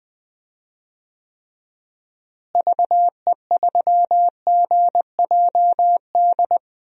Morse code, heard as a keyed tone, about 21 wpm, 700 Hz. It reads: VE3GJD
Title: DXCC-100 Call Signs - 6 Characters - Repeated 2x - 20wpm